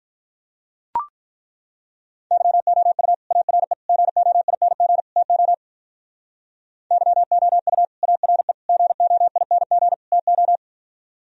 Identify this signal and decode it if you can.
Morse code, heard as a keyed tone, about 43 wpm, 700 Hz. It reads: YOU ARE GOING TO YOU ARE GOING TO
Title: Speed-Racing - Sets of 4 Words 35wpm